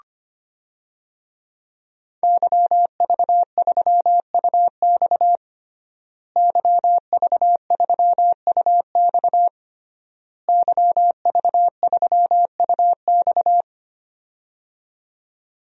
Morse code, heard as a keyed tone, about 25 wpm, 700 Hz. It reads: YV3UX YV3UX YV3UX
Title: DXCC-100 Call Signs - 5 Characters - Repeated 3x - 25wpm